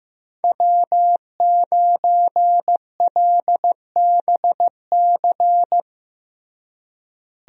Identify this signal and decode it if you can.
Morse code, heard as a keyed tone, about 15 wpm, 700 Hz. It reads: W9LBC
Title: USA Call Signs - 5 Characters - Repeated 3x - 15wpm